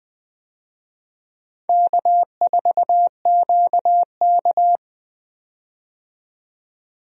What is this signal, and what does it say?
Morse code, heard as a keyed tone, about 20 wpm, 700 Hz. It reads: K4QK
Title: USA Call Signs - 4 Characters - 20wpm